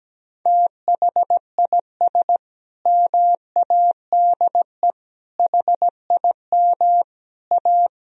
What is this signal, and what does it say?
Morse code, heard as a keyed tone, about 17 wpm, 700 Hz. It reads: THIS MADE HIM A
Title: Sentences from Top 1000 Words - Review Part 3 - 17wpm